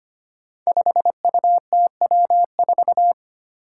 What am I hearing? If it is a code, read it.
Morse code, heard as a keyed tone, about 25 wpm, 700 Hz. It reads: 5UTW4
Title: Five Challenging Letter-Number Combinations 25wpm